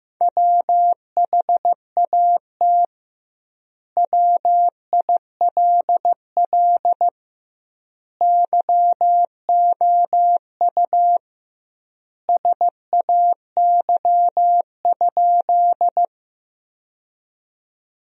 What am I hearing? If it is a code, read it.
Morse code, heard as a keyed tone, about 15 wpm, 700 Hz. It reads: WHAT WILL YOU SAY?
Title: 2x Word-Spacing - Sentences from Top 200 Words - 15wpm